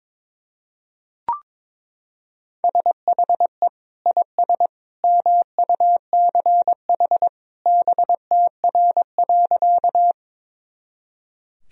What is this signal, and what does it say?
Morse code, heard as a keyed tone, about 22 wpm, 700 Hz. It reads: SHE IS MUCH BTR.
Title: Sentences from Top 400 Words and Abbreviations 22wpm